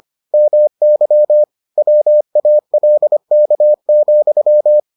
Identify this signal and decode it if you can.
Morse code, heard as a keyed tone, about 25 wpm, 600 Hz. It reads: MY WALK,